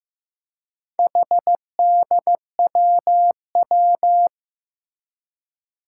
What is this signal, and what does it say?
Morse code, heard as a keyed tone, about 15 wpm, 700 Hz. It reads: HDWW
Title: Four Challenging Letter-Number Combinations 15wpm